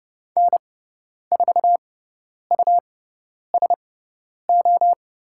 Morse code, heard as a keyed tone, about 30 wpm, spaced as for 10 wpm, 700 Hz. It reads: N4USO
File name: Lesson 18g - Letter m - Call Signs 30wpm